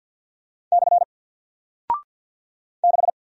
Morse code, heard as a keyed tone, about 48 wpm, 700 Hz. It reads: / 6
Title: Single Character (Letter:Number:Punctuation:Prosign) Alt 50wpm